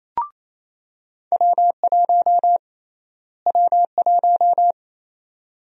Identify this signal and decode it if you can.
Morse code, heard as a keyed tone, about 27 wpm, 700 Hz. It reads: W1 W1
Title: USA Call Sign Prefixes - Repeated 2x - 28wpm